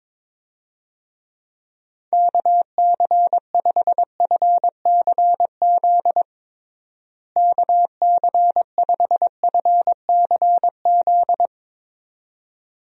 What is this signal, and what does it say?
Morse code, heard as a keyed tone, about 22 wpm, 700 Hz. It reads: KC5FCZ KC5FCZ
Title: USA Call Signs - 6 Characters - Repeated 2x - 22wpm